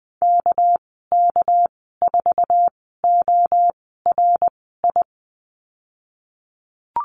Morse code, heard as a keyed tone, about 20 wpm, spaced as for 15 wpm, 700 Hz. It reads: KK4ORI
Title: 200 Lesson 32g - Letter k - Call Signs 20wpm